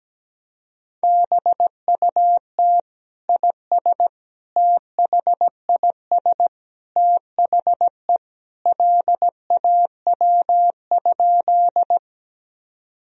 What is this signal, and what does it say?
Morse code, heard as a keyed tone, about 17 wpm, 700 Hz. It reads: BUT IS THIS THE LAW?
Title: Sentences from Top 1000 Words - Review Part 3 - 17wpm